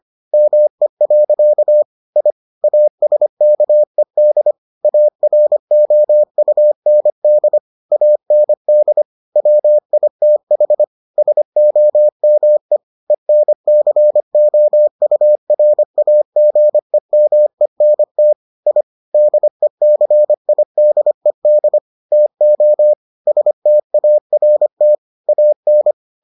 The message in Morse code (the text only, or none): ME. I ASKED AROUND AND WITH SOME ENCOURAGEMENT I DECIDED TO START AN